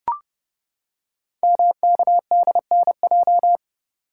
Morse code, heard as a keyed tone, about 30 wpm, 700 Hz. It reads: MKDNJ
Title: Code Groups - 5 Letters - 30wpm